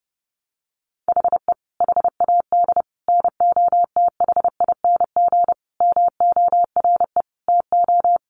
Morse code, heard as a keyed tone, about 30 wpm, 700 Hz. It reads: HE HAD NOTHING MORE TO
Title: Sentences from Top 1100 Words 30wpm